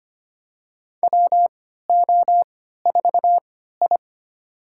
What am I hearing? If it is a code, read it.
Morse code, heard as a keyed tone, about 25 wpm, spaced as for 14 wpm, 700 Hz. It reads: WO4I
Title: USA Call Signs - 4 Characters - 3x Character Spacing - 25wpm